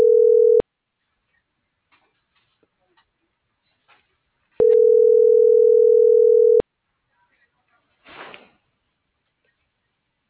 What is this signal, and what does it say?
Ringback tone (North America), 440 + 480 Hz.